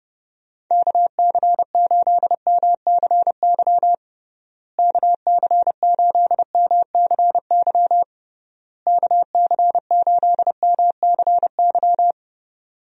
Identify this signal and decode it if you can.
Morse code, heard as a keyed tone, about 32 wpm, 700 Hz. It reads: KC8MCY KC8MCY KC8MCY
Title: USA Call Signs - 6 Characters - Repeated 3x - 30wpm